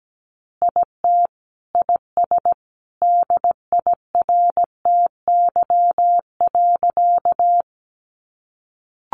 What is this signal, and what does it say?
Morse code, heard as a keyed tone, about 17 wpm, 700 Hz. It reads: IT IS DIRTY.
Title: Additional Easy Phrases 17wpm